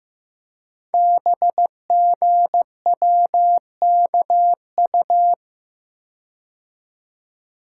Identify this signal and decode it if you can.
Morse code, heard as a keyed tone, about 15 wpm, 700 Hz. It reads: BGWKU
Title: Code Groups - 5 Letters - 15wpm